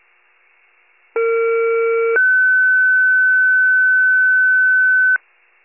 Two-tone page, heard as a one-second tone, then a three-second tone, about 460 then 1500 Hz. It is heard from 1.1 to 5.2 s.